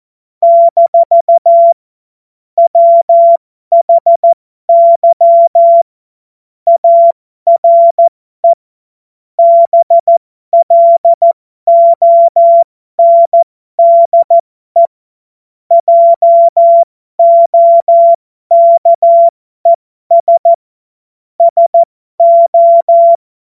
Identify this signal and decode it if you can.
Morse code, heard as a keyed tone, about 14 wpm, spaced as for 12 wpm, 680 Hz. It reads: - WHY ARE BLONDE JOKES SO
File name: Morsekurs / Witze